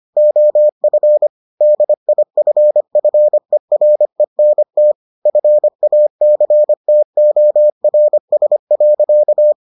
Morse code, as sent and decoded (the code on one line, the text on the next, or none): --- ..-. / -.. .. ..-. ..-. .-. -. - / ..-. .- -.-. - --- .-. ... .-.-.-
OF DIFFERENT FACTORS.